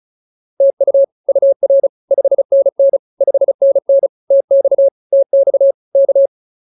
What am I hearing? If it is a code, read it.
Morse code, heard as a keyed tone, about 34 wpm, 550 Hz. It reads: TU UR 5NN 5NN TX TX K